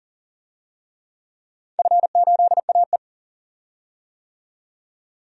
Morse code, heard as a keyed tone, about 40 wpm, 700 Hz. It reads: F8AE
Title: DXCC-100 Call Signs - 4 Characters - 40wpm